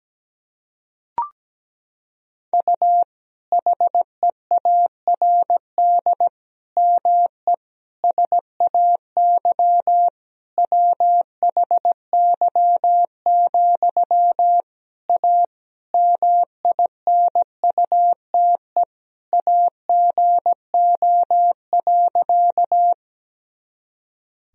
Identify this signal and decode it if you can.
Morse code, heard as a keyed tone, about 17 wpm, 700 Hz. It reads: U HEARD ME SAY WHY, A MINUTE AGO.
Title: Sentences from Top 500 Words and Abbreviations 17wpm